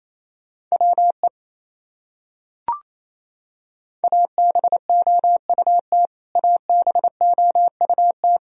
Morse code, heard as a keyed tone, about 28 wpm, 700 Hz. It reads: WE ABOUT ABOUT
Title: Top 100 Words Repeated 28wpm